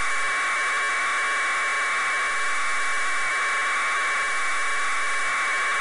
Packet radio (AX.25, as in APRS).